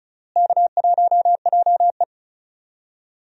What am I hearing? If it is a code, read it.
Morse code, heard as a keyed tone, about 36 wpm, 700 Hz. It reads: K1JE
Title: USA Call Signs - 4 Characters - Repeated 3x - 35wpm